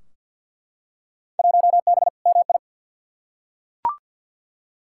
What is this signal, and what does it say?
Morse code, heard as a keyed tone, about 49 wpm, 700 Hz. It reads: 1B MI
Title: Field Day Exchanges - 50wpm